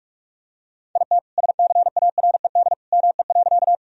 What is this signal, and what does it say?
Morse code, heard as a keyed tone, about 43 wpm, 700 Hz. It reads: IT SCARED ME.
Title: Sentences from Top 1000 Words - Review Part 4 - 45wpm